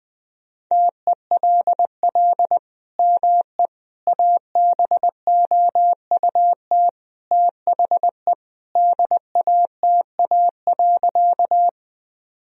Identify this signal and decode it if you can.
Morse code, heard as a keyed tone, about 20 wpm, 700 Hz. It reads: TELL ME ABOUT THE DATA.
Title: Sentences from Top 1000 Words - Review Part 3 - 20wpm